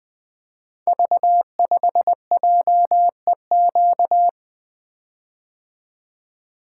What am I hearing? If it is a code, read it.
Morse code, heard as a keyed tone, about 20 wpm, 700 Hz. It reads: V5JEQ
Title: Code Groups - 5 Letter:Numbers - 20wpm